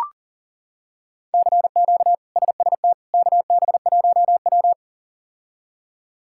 Morse code, heard as a keyed tone, about 40 wpm, 700 Hz. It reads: CQ SST KB1W
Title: K1USN SST QSO Practice - 40wpm